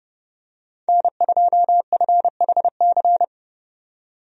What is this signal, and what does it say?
Morse code, heard as a keyed tone, about 30 wpm, 700 Hz. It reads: N2FHC
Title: USA Call Signs - 5 Characters - 30wpm